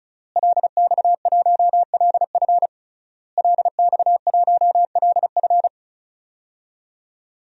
Morse code, heard as a keyed tone, about 36 wpm, 700 Hz. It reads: LX1LF LX1LF
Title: DXCC-100 Call Signs - 5 Characters - Repeated 3x - 35wpm